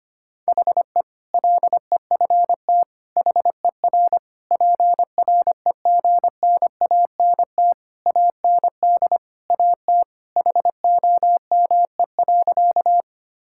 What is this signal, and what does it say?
Morse code, heard as a keyed tone, about 25 wpm, 700 Hz. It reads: HE LEFT HER PREGNANT AND AT HOME.
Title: Sentences from Top 1000 Words - Review Part 6 - 25wpm